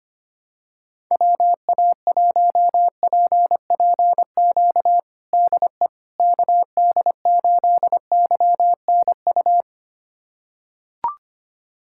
Morse code, heard as a keyed tone, about 25 wpm, 700 Hz. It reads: WA1PPQ DE KD8YNU
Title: USA Callsign DE Callsign - 6 Characters - 25wpm